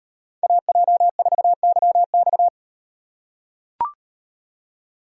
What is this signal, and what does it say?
Morse code, heard as a keyed tone, about 37 wpm, 700 Hz. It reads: AJ4YX